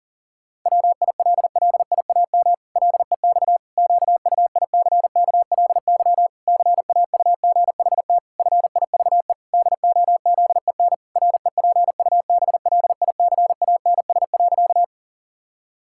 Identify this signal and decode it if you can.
Morse code, heard as a keyed tone, about 40 wpm, 700 Hz. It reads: WILLIAM LEX QUICKLY CAUGHT FIVE DOZEN REPUBLICANS.
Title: Pangram Challenge v2 - 40wpm